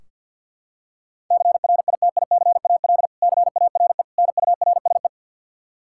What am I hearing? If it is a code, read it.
Morse code, heard as a keyed tone, about 47 wpm, 700 Hz. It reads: CRITICAL CARE NURSE
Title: Practice QSO - Exchange 3 - More Info - 50wpm